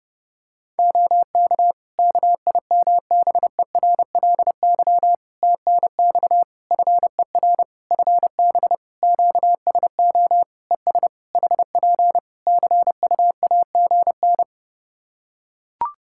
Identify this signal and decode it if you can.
Morse code, heard as a keyed tone, about 30 wpm, 700 Hz. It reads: OK KIMBERLY TNX FER FB QSO ES HP CUAGN
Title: Practice QSO - Exchange 4 - Ending - 30wpm